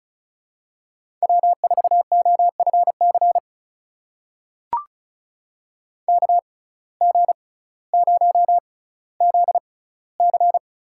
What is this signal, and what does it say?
Morse code, heard as a keyed tone, about 35 wpm, spaced as for 18 wpm, 700 Hz. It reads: W4OFC KG0ZC
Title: USA Call Signs - 5 Characters - 6x Character Spacing - 35wpm